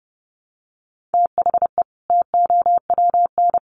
Morse code, heard as a keyed tone, about 30 wpm, 700 Hz. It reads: THE TOWN
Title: Sentences from Top 1300 Words 30wpm